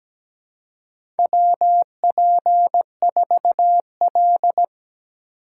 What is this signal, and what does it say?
Morse code, heard as a keyed tone, about 17 wpm, 700 Hz. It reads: WP4L
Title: USA Call Signs - 3x Character Spacing - 17wpm